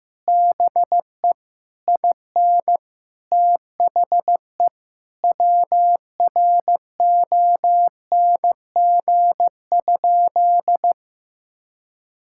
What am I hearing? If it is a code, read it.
Morse code, heard as a keyed tone, about 15 wpm, 700 Hz. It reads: BE IN THE WRONG?
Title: Sentences from Top 1000 Words - Review Part 6 - 15wpm